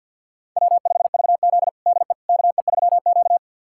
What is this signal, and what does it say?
Morse code, heard as a keyed tone, about 47 wpm, 700 Hz. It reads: W5VZ DE KE3Q
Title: USA Callsign DE Callsign - 4 Characters - 50wpm